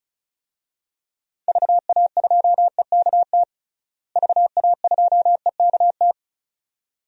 Morse code, heard as a keyed tone, about 36 wpm, 700 Hz. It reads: VA2EKT VA2EKT
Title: DXCC-100 Call Signs - 6 Characters - Repeated 2x - 35wpm